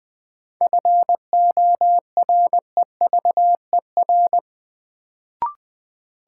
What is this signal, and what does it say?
Morse code, heard as a keyed tone, about 20 wpm, 700 Hz. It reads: FOREVER
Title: Sentences from Top 1600 Words - New Word Drill - 20wpm